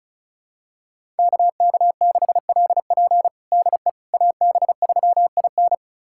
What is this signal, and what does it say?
Morse code, heard as a keyed tone, about 34 wpm, 690 Hz. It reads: KK6LP DE AB3IN
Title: USA Callsign DE Callsign - 5 Characters - 35wpm